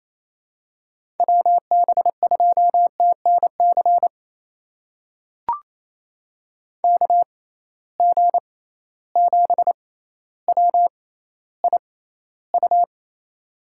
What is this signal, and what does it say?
Morse code, heard as a keyed tone, about 28 wpm, spaced as for 14 wpm, 700 Hz. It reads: WB2TNC KG7WIU
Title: USA Call Signs - 6 Characters - 6x Character Spacing - 28wpm